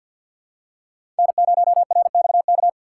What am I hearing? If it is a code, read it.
Morse code, heard as a keyed tone, about 51 wpm, 700 Hz. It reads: N0RXK
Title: USA Call Signs - 5 Characters - 3x Character Spacing - 50wpm